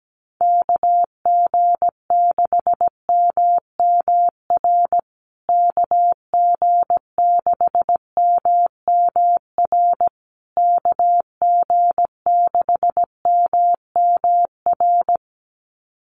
Morse code, heard as a keyed tone, about 17 wpm, 700 Hz. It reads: KG6MMR KG6MMR KG6MMR
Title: USA Call Signs Repeated 3x 17wpm